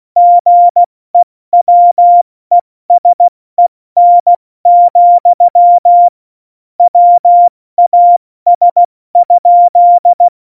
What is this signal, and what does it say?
Morse code, heard as a keyed tone, about 16 wpm, 700 Hz. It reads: GEWESEN, WAS?
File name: Morsekurs / Witze / Witz 28. November 2017, 70 F80